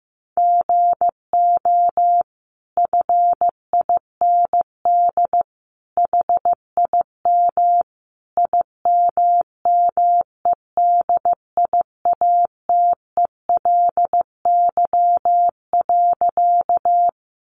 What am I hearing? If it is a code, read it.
Morse code, heard as a keyed tone, about 14 wpm, 700 Hz. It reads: GO FIND HIM IMMEDIATELY.
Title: Sentences from Top 1100 Words 15wpm